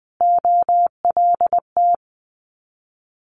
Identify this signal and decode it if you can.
Morse code, heard as a keyed tone, about 20 wpm, 700 Hz. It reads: OLT
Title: Multi-letter Phonemes Repeated 3x 20wpm